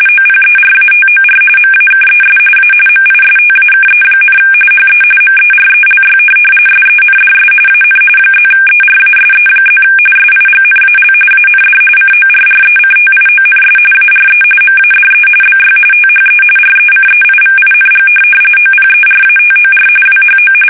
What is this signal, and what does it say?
RTTY, about 45.45 baud. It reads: OPERATOR MAKES A MISTAKE, HE MERELY SENDS AN +ERASE+ SIGNAL AND CORRECTIONS ARE MADE BY THE RECEIVING OPERATOR WHO
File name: fsk850